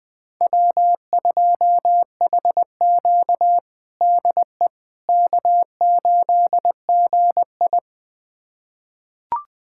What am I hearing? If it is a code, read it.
Morse code, heard as a keyed tone, about 20 wpm, 700 Hz. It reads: W2HQ DE K8GI
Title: USA Callsign DE Callsign - 4 Characters - 20wpm